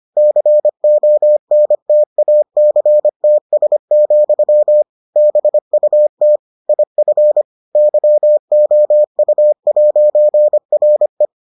Morse code, heard as a keyed tone, about 25 wpm, 600 Hz. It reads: CONTACTS, BUT IF YOU'RE